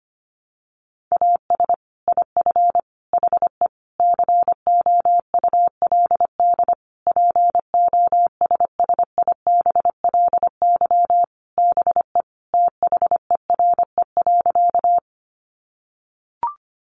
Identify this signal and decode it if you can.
Morse code, heard as a keyed tone, about 25 wpm, 700 Hz. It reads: AS IF HE COULD POSSIBLY BE THERE.
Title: Sentences from Top 1300 Words 25wpm